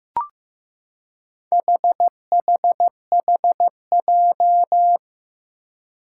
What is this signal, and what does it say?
Morse code, heard as a keyed tone, about 15 wpm, 700 Hz. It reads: HHHJ
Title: Four Challenging Letter-Number Combinations 15wpm